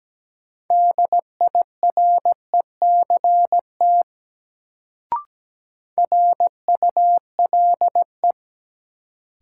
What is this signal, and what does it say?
Morse code, heard as a keyed tone, about 17 wpm, 700 Hz. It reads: DIRECT RULE
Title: Sentences from Top 400 Words - New Word Drill - 17wpm